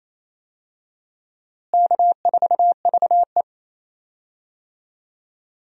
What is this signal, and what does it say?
Morse code, heard as a keyed tone, about 28 wpm, 700 Hz. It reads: K4VE
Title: USA Call Signs - 4 Characters - Repeated 2x - 28wpm